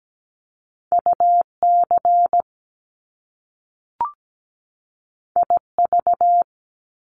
Morse code, heard as a keyed tone, about 17 wpm, 700 Hz. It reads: UC IV
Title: Top Two Letter Combinations in Words 17wpm